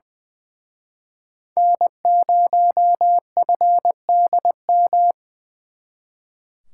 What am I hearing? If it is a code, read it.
Morse code, heard as a keyed tone, about 24 wpm, 700 Hz. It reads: N0FDM
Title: Speed Racing - USA 1x3 Call Signs 20wpm